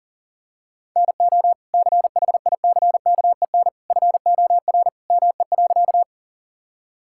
Morse code, heard as a keyed tone, about 40 wpm, 700 Hz. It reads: NO CHICKEN FOR ME.
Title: Sentences from Top 1000 Words - Review Part 5 - 40wpm